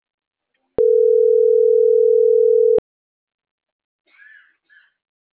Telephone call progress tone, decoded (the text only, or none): ringback tone